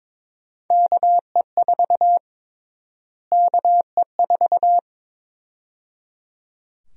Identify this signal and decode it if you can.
Morse code, heard as a keyed tone, about 22 wpm, 700 Hz. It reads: KE4 KE4
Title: USA Call Sign Prefixes - Repeated 2x - 22wpm